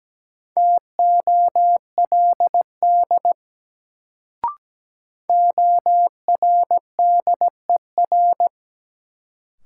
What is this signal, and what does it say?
Morse code, heard as a keyed tone, about 17 wpm, 700 Hz. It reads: TOLD ORDER
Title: Sentences from Top 400 Words - New Word Drill - 17wpm